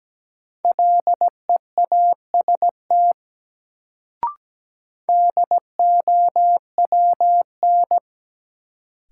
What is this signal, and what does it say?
Morse code, heard as a keyed tone, about 17 wpm, 700 Hz. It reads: LEAST DOWN
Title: Sets of 2 Words - New Word Drill - 17wpm